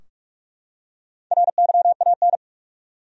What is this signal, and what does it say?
Morse code, heard as a keyed tone, about 48 wpm, 700 Hz. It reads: RYAN
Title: USA Common Names - Max 5 Letters - 45wpm